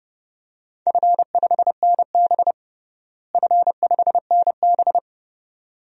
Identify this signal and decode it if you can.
Morse code, heard as a keyed tone, about 30 wpm, 700 Hz. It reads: F5NB F5NB
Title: DXCC-100 Call Signs - 4 Characters - Repeated 2x - 30wpm